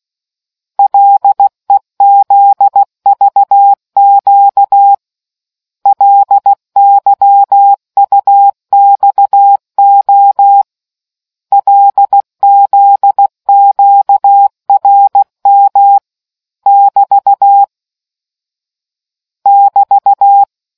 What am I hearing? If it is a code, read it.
Morse code, heard as a keyed tone, about 16 wpm, 800 Hz. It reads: LEZVQ LYUXO LZQRM <BT> <BT>